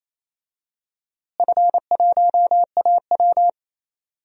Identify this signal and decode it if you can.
Morse code, heard as a keyed tone, about 28 wpm, 700 Hz. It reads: F1AW
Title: DXCC-100 Call Signs - 4 Characters - Repeated 2x - 28wpm